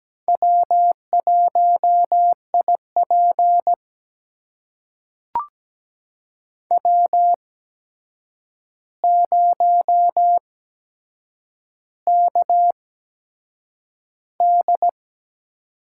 Morse code, heard as a keyed tone, about 17 wpm, spaced as for 7 wpm, 700 Hz. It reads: W1IP W0KD
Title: USA Call Signs - 4 Characters - 8x Character Spacing - 17wpm